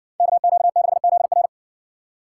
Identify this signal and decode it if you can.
Morse code, heard as a keyed tone, about 60 wpm, 700 Hz. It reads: KQ6ZR